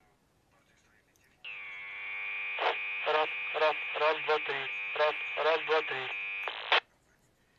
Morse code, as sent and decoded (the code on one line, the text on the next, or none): none